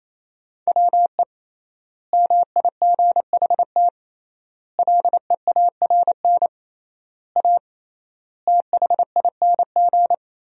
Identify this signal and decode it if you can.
Morse code, heard as a keyed tone, about 28 wpm, 700 Hz. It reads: WE MIGHT LEARN A THING
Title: 3x Word-Spacing - Sentences from Top 300 Words - 28wpm